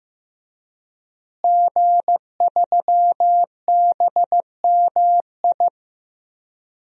Morse code, heard as a keyed tone, about 16 wpm, 700 Hz. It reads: G3BMI
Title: DXCC-100 Call Signs - 5 Characters - Repeated 2x - 15wpm